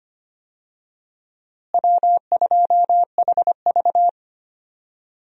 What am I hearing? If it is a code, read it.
Morse code, heard as a keyed tone, about 25 wpm, 700 Hz. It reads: W2HV
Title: USA Call Signs - 4 Characters - Repeated 3x - 25wpm